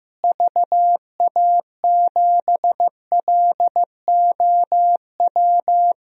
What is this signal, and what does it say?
Morse code, heard as a keyed tone, about 15 wpm, 700 Hz. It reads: VA7LOW
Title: DXCC-100 Call Signs - 6 Characters - Repeated 3x - 15wpm